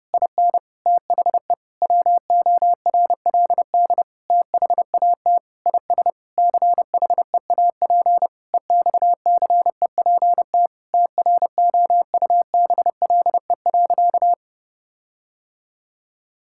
Morse code, heard as a keyed tone, about 30 wpm, 700 Hz. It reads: IN THE WORLD THAT IS CHEAP EXCEPT TROUBLE.
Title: Sentences from Top 1600 Words 30wpm